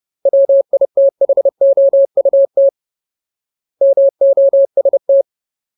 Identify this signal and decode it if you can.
Morse code, heard as a keyed tone, about 30 wpm, 550 Hz. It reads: WITHOUT MOST
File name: Ultimate Word List / 30WPM Ultimate